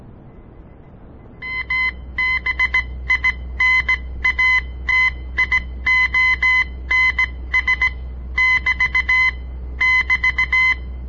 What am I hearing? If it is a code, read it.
Morse code, heard as a keyed tone, about 17 wpm, spaced as for 14 wpm, 2050 Hz. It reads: ABINATIONS<BT><BT>
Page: On the air Morse code practice from the Southern Montana Amatuer Radio Association; all code, no read back